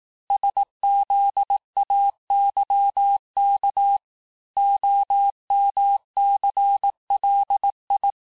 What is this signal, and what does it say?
Morse code, heard as a keyed tone, about 18 wpm, 800 Hz. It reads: SZAYK OMCLI
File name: cours-du-lundi files / cours-1852-texte-chiffre-vitesse-17MPM